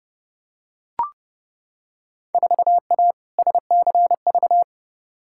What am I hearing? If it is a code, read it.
Morse code, heard as a keyed tone, about 30 wpm, 700 Hz. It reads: 4A SCV
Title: Field Day Exchanges - 30wpm